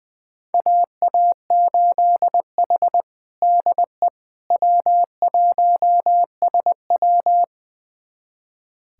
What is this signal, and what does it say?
Morse code, heard as a keyed tone, about 20 wpm, 700 Hz. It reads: AA8H DE W1SW
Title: USA Callsign DE Callsign - 4 Characters - 20wpm